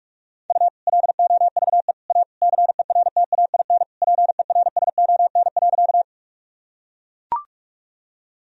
Morse code, heard as a keyed tone, about 41 wpm, 700 Hz. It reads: U LOVE A CERTAIN PERSON.
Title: Sentences from Top 500 Words and Abbreviations 45wpm